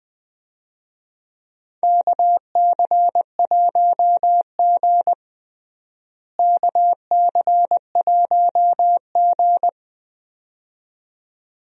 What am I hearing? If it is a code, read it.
Morse code, heard as a keyed tone, about 21 wpm, 700 Hz. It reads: KC1G KC1G
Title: USA Call Signs - 4 Characters - Repeated 2x - 20wpm